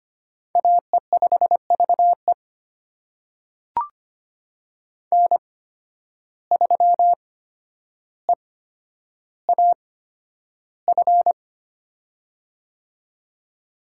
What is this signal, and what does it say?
Morse code, heard as a keyed tone, about 25 wpm, spaced as for 11 wpm, 700 Hz. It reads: AE5VE N3EAF
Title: USA Call Signs - 5 Characters - 8x Character Spacing - 25wpm